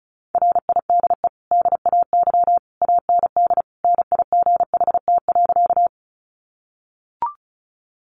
Morse code, heard as a keyed tone, about 34 wpm, 700 Hz. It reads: RIDE DAY AND NIGHT.
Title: Sentences from Top 1200 Words 35wpm